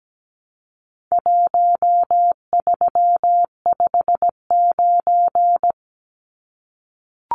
Morse code, heard as a keyed tone, about 17 wpm, 700 Hz. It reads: 1359
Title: Numbers - Contest Style - 4-Digit 17wpm